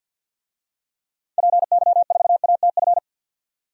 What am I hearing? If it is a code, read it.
Morse code, heard as a keyed tone, about 52 wpm, 700 Hz. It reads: PY4ATF
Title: DXCC-100 Call Signs - 6 Characters - 50wpm